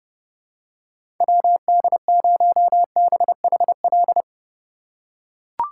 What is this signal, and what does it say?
Morse code, heard as a keyed tone, about 32 wpm, 700 Hz. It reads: WD0BHL
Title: USA Call Signs - 3x Character Spacing - 30wpm